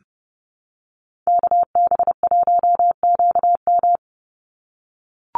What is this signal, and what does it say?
Morse code, heard as a keyed tone, about 30 wpm, 700 Hz. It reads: KB1QM
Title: USA Call Signs 30wpm